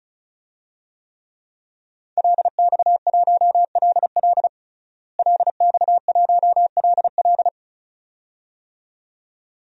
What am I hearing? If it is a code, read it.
Morse code, heard as a keyed tone, about 34 wpm, 700 Hz. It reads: LX1LL LX1LL
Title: DXCC-100 Call Signs - 5 Characters - Repeated 2x - 35wpm